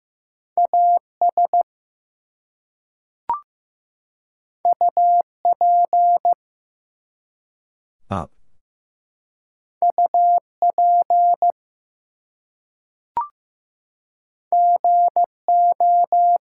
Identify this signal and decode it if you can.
Morse code, heard as a keyed tone, about 15 wpm, 700 Hz. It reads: AS UP UP GO
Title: Two Letter Words - 15wpm